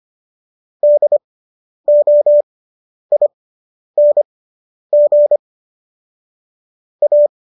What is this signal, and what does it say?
Morse code, heard as a keyed tone, about 25 wpm, spaced as for 10 wpm, 590 Hz. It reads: DOING A